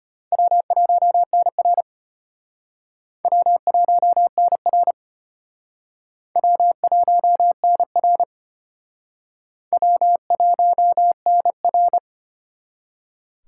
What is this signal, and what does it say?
Morse code, heard as a keyed tone, about 32 wpm, 700 Hz. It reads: W1NR W1NR W1NR W1NR